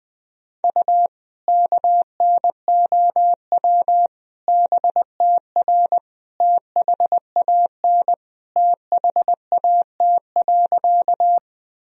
Morse code, heard as a keyed tone, about 20 wpm, 700 Hz. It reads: U KNOW BTR THAN THAT.
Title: Sentences from Top 400 Words and Abbreviations 20wpm